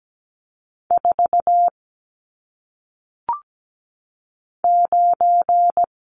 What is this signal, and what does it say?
Morse code, heard as a keyed tone, about 17 wpm, 700 Hz. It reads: 4 9
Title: Numbers - 1-Digit 17wpm